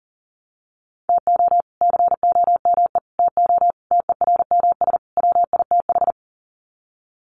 Morse code, heard as a keyed tone, about 37 wpm, 700 Hz. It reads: TO COME TO TERMS WITH
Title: Sets of 5 Words Encore 40wpm